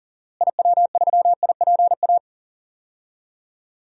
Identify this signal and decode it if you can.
Morse code, heard as a keyed tone, about 40 wpm, 700 Hz. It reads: IW3IPA
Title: DXCC-100 Call Signs - 6 Characters - Repeated 3x - 40wpm